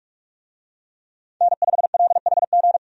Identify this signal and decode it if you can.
Morse code, heard as a keyed tone, about 46 wpm, 700 Hz. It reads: N5LHG
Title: USA Call Signs - 5 Characters - 45wpm